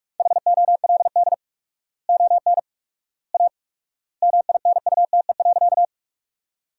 Morse code, heard as a keyed tone, about 45 wpm, 690 Hz. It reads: HOLD ON A MINUTE.
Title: 4x Word-Spacing - Sentences from Top 500 Words - 45wpm